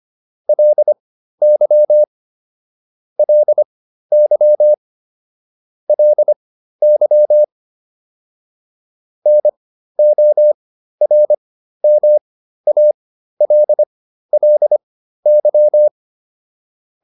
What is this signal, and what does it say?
Morse code, heard as a keyed tone, about 25 wpm, spaced as for 13 wpm, 600 Hz. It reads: LY LY LY NORMALLY